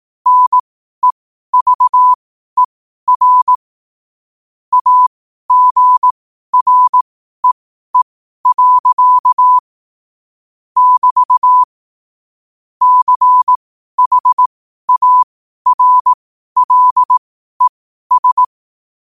Morse code, heard as a keyed tone, about 18 wpm, spaced as for 12 wpm, 1000 Hz. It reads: NEVER AGREE. <BT> CHARLES
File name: qotdfiles / 12wpm